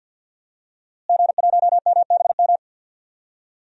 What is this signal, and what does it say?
Morse code, heard as a keyed tone, about 52 wpm, 700 Hz. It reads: G1MBM
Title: DXCC-100 Call Signs - 5 Characters - Repeated 2x - 50wpm